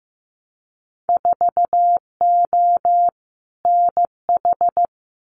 Morse code, heard as a keyed tone, about 15 wpm, 700 Hz. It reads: 4O NH
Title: Winter Field Day Exchanges Encore 15wpm